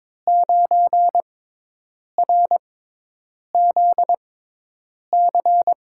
Morse code, heard as a keyed tone, about 23 wpm, spaced as for 8 wpm, 700 Hz. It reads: 9RZC
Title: USA Call Signs - 5 Characters - 6x Character Spacing - 22wpm